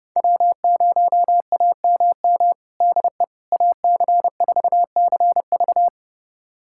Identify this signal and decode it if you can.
Morse code, heard as a keyed tone, about 30 wpm, 700 Hz. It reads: W0AMM DE AC4CV
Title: USA Callsign DE Callsign 30wpm